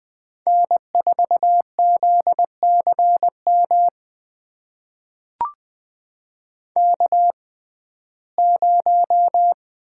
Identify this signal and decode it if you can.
Morse code, heard as a keyed tone, about 20 wpm, spaced as for 15 wpm, 700 Hz. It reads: N4ZCM K0
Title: USA Call Signs - 5 Characters - 6x Character Spacing - 20wpm